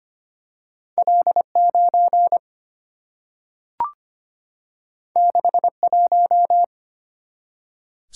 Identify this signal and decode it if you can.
Morse code, heard as a keyed tone, about 25 wpm, 700 Hz. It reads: L9 61
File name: Two Letter-Number Combinations - 25wpm